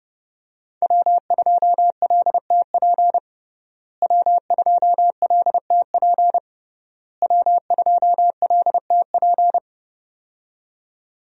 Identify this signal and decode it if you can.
Morse code, heard as a keyed tone, about 32 wpm, 700 Hz. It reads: W2LTP W2LTP W2LTP
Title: USA Call Signs - 5 Characters - Repeated 3x - 30wpm